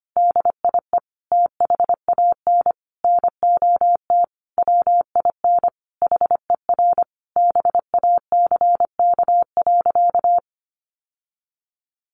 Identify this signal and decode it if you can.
Morse code, heard as a keyed tone, about 25 wpm, 700 Hz. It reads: DIE THAN NOT WIN HER BACK.
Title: Sentences from Top 1500 Words 25wpm